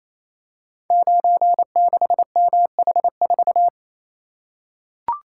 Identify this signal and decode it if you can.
Morse code, heard as a keyed tone, about 27 wpm, 700 Hz. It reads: 96MH4
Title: Code Groups - 5 Letter:Numbers - 28wpm